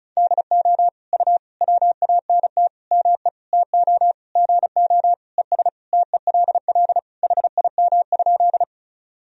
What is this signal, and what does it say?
Morse code, heard as a keyed tone, about 34 wpm, 690 Hz. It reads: DO U WANT ME TO GO ES TELL HIM?
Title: Sentences from Top 200 Words and Abbreviations 35wpm